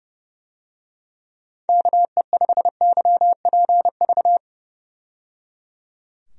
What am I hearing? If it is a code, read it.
Morse code, heard as a keyed tone, about 30 wpm, 700 Hz. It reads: KE5YPV
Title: USA Call Signs - 6 Characters - 30wpm